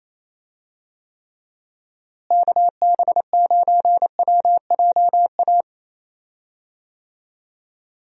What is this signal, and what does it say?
Morse code, heard as a keyed tone, about 28 wpm, 700 Hz. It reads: KB9WJA
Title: USA Call Signs - 6 Characters - Repeated 3x - 28wpm